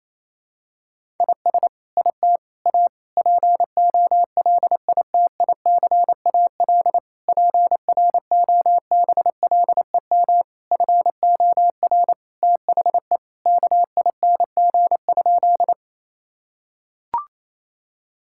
Morse code, heard as a keyed tone, about 28 wpm, 700 Hz. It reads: IS IT A POLITICAL PROBLEM FOR THE KING?
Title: Sentences from Top 1000 Words - Review Part 5 - 28wpm